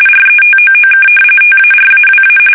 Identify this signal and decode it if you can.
RTTY, about 45.45 baud. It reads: RETOOK TARAWA
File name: fsk850 / battleshipnj